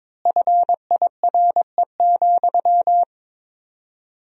Morse code, heard as a keyed tone, about 22 wpm, 700 Hz. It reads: FIRE,
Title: 4x Word-Spacing - Sentences from Top 400 Words - 22wpm